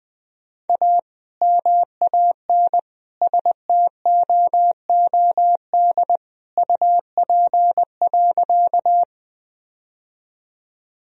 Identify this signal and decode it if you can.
Morse code, heard as a keyed tone, about 20 wpm, 700 Hz. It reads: A MAN STOOD UP.
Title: Sentences from Top 1000 Words - Review Part 2 - 20wpm